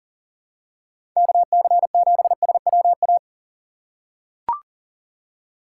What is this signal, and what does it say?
Morse code, heard as a keyed tone, about 40 wpm, 700 Hz. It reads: KC7SWA
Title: USA Call Signs - 6x Character Spacing - 40wpm